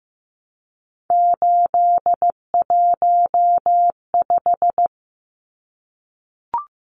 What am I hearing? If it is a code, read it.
Morse code, heard as a keyed tone, about 15 wpm, 700 Hz. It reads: 815
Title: Numbers - 3-Digit 15wpm